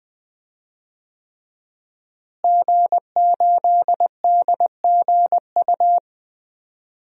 Morse code, heard as a keyed tone, about 20 wpm, 700 Hz. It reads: G8DGU
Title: DXCC-100 Call Signs - 5 Characters - Repeated 3x - 20wpm